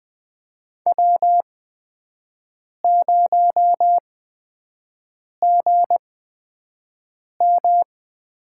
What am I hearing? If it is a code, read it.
Morse code, heard as a keyed tone, about 20 wpm, spaced as for 5 wpm, 700 Hz. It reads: W0GM
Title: USA Call Signs - 8x Character Spacing - 20wpm